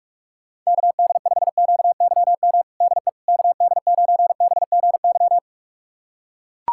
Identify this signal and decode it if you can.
Morse code, heard as a keyed tone, about 41 wpm, 700 Hz. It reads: KD5QYM DE KD9BGY
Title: USA Callsign DE Callsign - 6 Characters - 45wpm